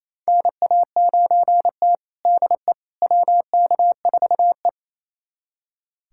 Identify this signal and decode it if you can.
Morse code, heard as a keyed tone, about 28 wpm, 700 Hz. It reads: NA9T DE WK4E
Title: USA Callsign DE Callsign - 4 Characters - 28wpm